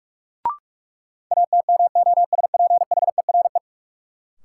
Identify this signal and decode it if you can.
Morse code, heard as a keyed tone, about 46 wpm, 700 Hz. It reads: ATMOSPHERE
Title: Sentences from Top 1600 Words - New Word Drill - 45wpm